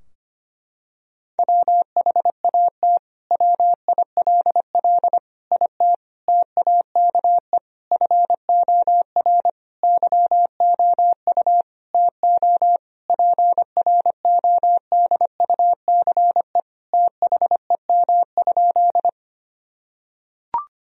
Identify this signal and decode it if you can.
Morse code, heard as a keyed tone, about 25 wpm, 700 Hz. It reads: WHAT WILL IT TAKE FOR YOU TO PRODUCE THEM?
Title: Sentences from Top 1000 Words - Review Part 3 - 25wpm